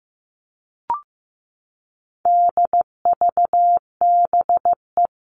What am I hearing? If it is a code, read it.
Morse code, heard as a keyed tone, about 15 wpm, 700 Hz. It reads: DVBE
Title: All Cut Numbers - 4-Digit 15wpm